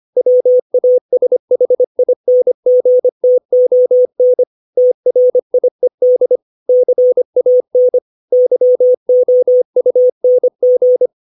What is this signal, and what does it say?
Morse code, heard as a keyed tone, about 25 wpm, 500 Hz. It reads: WASHINGTON TRIED CAN YOUNG
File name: 632 words / 632 Words 25WPM